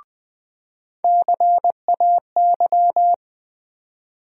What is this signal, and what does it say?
Morse code, heard as a keyed tone, about 20 wpm, 700 Hz. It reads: CAY
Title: USA Call Sign Postfixes - 3 Letters 20wpm